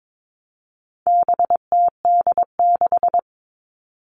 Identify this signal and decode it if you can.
Morse code, heard as a keyed tone, about 22 wpm, 700 Hz. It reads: BTD6
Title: All Cut Numbers - 4-Digit 22wpm